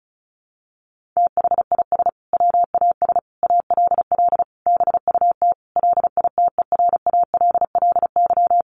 Morse code, heard as a keyed tone, about 34 wpm, 700 Hz. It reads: THIS WAS ALL BUT LITERALLY
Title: Sentences from Top 1400 Words 35wpm